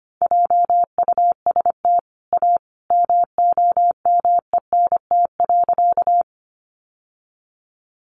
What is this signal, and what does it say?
Morse code, heard as a keyed tone, about 25 wpm, 700 Hz. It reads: JUST A MOMENT.
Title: Sentences from Top 1000 Words 25wpm